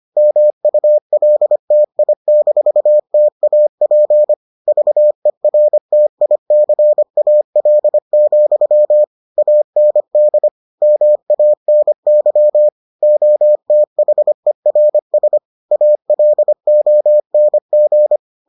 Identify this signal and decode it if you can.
Morse code, heard as a keyed tone, about 25 wpm, 600 Hz. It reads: MULTI-TAP VERTICAL, AND MANY OTHERS ALONG